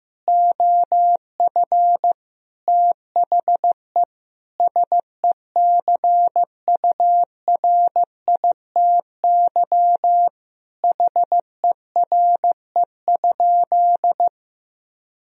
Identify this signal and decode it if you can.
Morse code, heard as a keyed tone, about 15 wpm, 700 Hz. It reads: OF THE SECURITY HERE?
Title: Sentences from Top 1000 Words - Review Part 4 - 15wpm